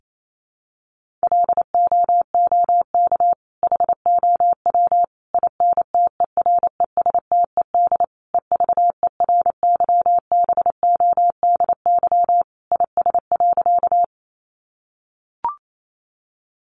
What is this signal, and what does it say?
Morse code, heard as a keyed tone, about 28 wpm, 700 Hz. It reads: LOOK HOW INTERESTED EVERYBODY IS.
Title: Sentences from Top 1400 Words 28wpm